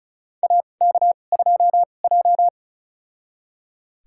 Morse code, heard as a keyed tone, about 34 wpm, spaced as for 25 wpm, 700 Hz. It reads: AK2J
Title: USA Call Signs - 2x Character Spacing - 35wpm